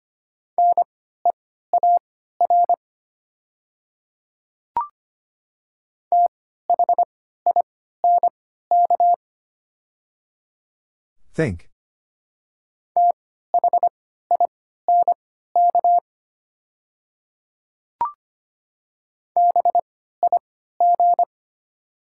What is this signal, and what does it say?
Morse code, heard as a keyed tone, about 25 wpm, spaced as for 14 wpm, 700 Hz. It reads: NEAR THINK THINK BIG
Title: 3x Character Spacing - Top 300 Words - 25wpm